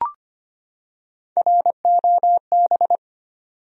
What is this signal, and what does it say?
Morse code, heard as a keyed tone, about 25 wpm, 700 Hz. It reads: ROB
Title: USA Common Names - Max 3 Letters - 25wpm